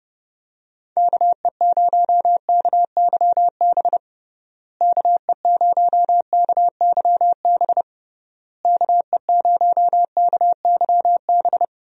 Morse code, heard as a keyed tone, about 30 wpm, 700 Hz. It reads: KE0KYB KE0KYB KE0KYB
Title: USA Call Signs - 6 Characters - Repeated 3x - 30wpm